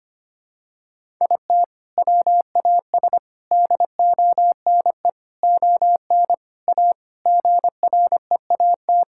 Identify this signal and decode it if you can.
Morse code, heard as a keyed tone, about 25 wpm, 700 Hz. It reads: IT WAS DONE ON A GREAT
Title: Sentences from Top 1000 Words - Review Part 5 - 25wpm